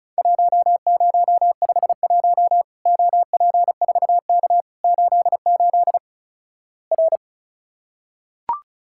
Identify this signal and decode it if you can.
Morse code, heard as a keyed tone, about 34 wpm, 700 Hz. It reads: 1051 OP4K 88 R
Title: Worked All Europe DX Contest - QTC Exchange - 35wpm